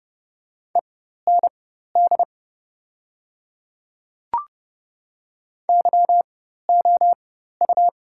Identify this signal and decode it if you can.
Morse code, heard as a keyed tone, about 30 wpm, spaced as for 14 wpm, 700 Hz. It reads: END YOU
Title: 4x Character Spacing - Top 300 Words - 30wpm